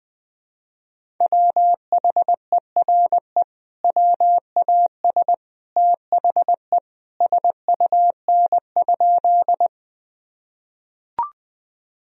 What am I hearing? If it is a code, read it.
Morse code, heard as a keyed tone, about 20 wpm, 700 Hz. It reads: WHERE WAS THE SUN?
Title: Sentences from Top 1000 Words - Review Part 1 - 20wpm